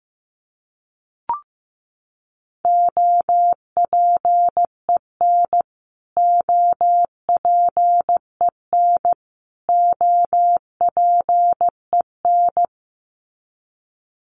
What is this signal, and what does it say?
Morse code, heard as a keyed tone, about 15 wpm, 700 Hz. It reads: OPEN OPEN OPEN
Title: Top 500 Words Repeated 15wpm